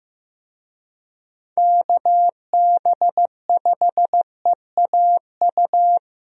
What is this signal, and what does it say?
Morse code, heard as a keyed tone, about 15 wpm, 700 Hz. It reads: KB5EAU
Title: USA Call Signs - 6 Characters - 15wpm